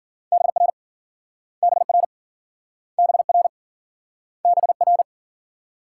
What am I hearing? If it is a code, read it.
Morse code, heard as a keyed tone, about 49 wpm, 700 Hz. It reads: BR BR BR BR
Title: Speed-Racing Multi-letter Phonemes - 40wpm